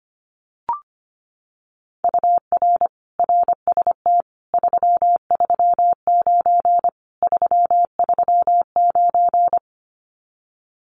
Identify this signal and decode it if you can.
Morse code, heard as a keyed tone, about 25 wpm, 700 Hz. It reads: UR RST 339 339
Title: Common QSO Phrases Encore 25wpm